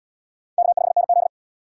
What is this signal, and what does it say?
Morse code, heard as a keyed tone, about 66 wpm, 700 Hz. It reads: K5NZ